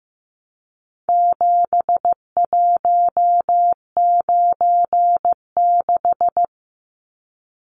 Morse code, heard as a keyed tone, about 15 wpm, 700 Hz. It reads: 7196
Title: Numbers - 4-Digit 15wpm